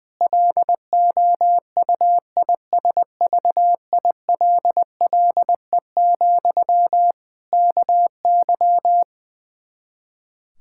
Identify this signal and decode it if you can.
Morse code, heard as a keyed tone, about 20 wpm, 700 Hz. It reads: LOUISVILLE, KY
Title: K1USN SST QSO Practice 20wpm